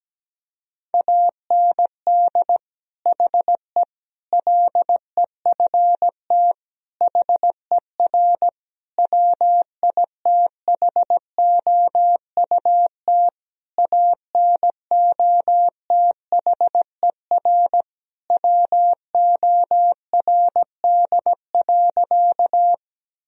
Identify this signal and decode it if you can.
Morse code, heard as a keyed tone, about 17 wpm, 700 Hz. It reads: AND HE LEFT HER WITHOUT ANOTHER WORD.
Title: Sentences from Top 1000 Words - Review Part 3 - 17wpm